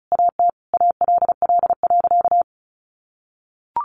Morse code, heard as a keyed tone, about 34 wpm, 700 Hz. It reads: AT ALL.
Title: Sentences from Top 1300 Words 35wpm